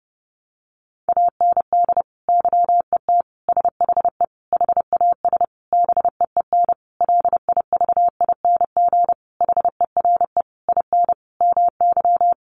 Morse code, heard as a keyed tone, about 30 wpm, 700 Hz. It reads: AND YET SHE HAS BEEN LIVING HERE IN MY
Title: Sentences from Top 1100 Words 30wpm